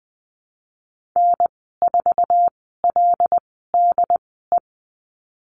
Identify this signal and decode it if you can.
Morse code, heard as a keyed tone, about 20 wpm, spaced as for 14 wpm, 700 Hz. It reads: N4LDE
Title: 076 Lesson 13g - Letter L - Call Signs 20wpm